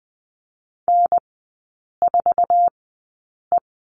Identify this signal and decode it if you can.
Morse code, heard as a keyed tone, about 20 wpm, spaced as for 8 wpm, 700 Hz. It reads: N4E
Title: 047 Lesson 9e - Number 4 - Three Characters 20wpm